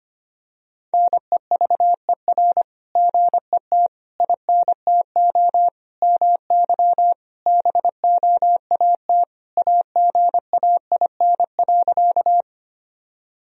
Morse code, heard as a keyed tone, about 25 wpm, 700 Hz. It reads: NEVER GET INTO MY BOAT AGAIN.
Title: Sentences from Top 1000 Words - Review Part 3 - 25wpm